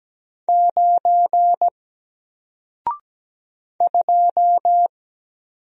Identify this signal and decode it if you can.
Morse code, heard as a keyed tone, about 17 wpm, 700 Hz. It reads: 9 2
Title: Single Letter-Number 17wpm